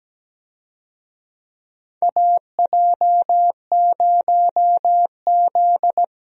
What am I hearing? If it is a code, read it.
Morse code, heard as a keyed tone, about 17 wpm, 700 Hz. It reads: AJ0Z
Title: USA Call Signs - 4 Characters - 17wpm